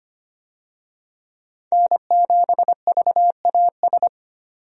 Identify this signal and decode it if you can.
Morse code, heard as a keyed tone, about 25 wpm, 700 Hz. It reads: N7VAS
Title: USA Call Signs - 5 Characters - 25wpm